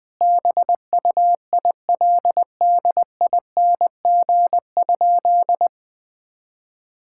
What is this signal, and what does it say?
Morse code, heard as a keyed tone, about 20 wpm, 700 Hz. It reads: BUILDING?
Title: Sentences from Top 1000 Words - Review Part 6 - 20wpm